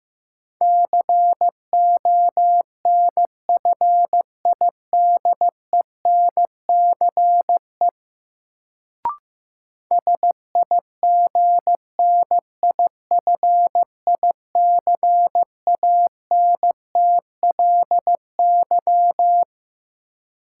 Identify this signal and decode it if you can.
Morse code, heard as a keyed tone, about 15 wpm, 700 Hz. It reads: CONFIDENCE SIGNIFICANTLY
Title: Sentences from Top 1500 Words - New Word Drill Alt - 15wpm